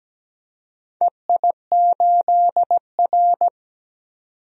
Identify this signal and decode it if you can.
Morse code, heard as a keyed tone, about 17 wpm, 700 Hz. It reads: EI8R
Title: DXCC-100 Call Signs - 4 Characters - 17wpm